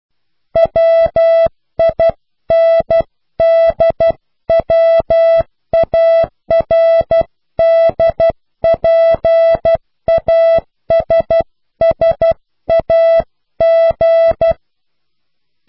Morse code, heard as a keyed tone, about 12 wpm, 660 Hz. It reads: WINDWARDPASSAG
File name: cw / Pseudo woorden (15)